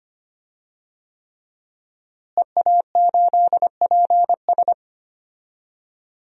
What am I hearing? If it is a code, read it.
Morse code, heard as a keyed tone, about 25 wpm, 700 Hz. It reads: EA8PS
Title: DXCC-100 Call Signs - 5 Characters - Repeated 3x - 25wpm